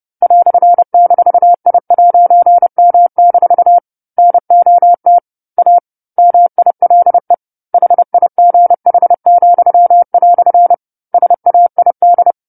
Morse code, heard as a keyed tone, about 30 wpm, 700 Hz. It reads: "-I'M- NOT A MILE HIGH," SAID
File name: 10-Chapter-10 30 WPM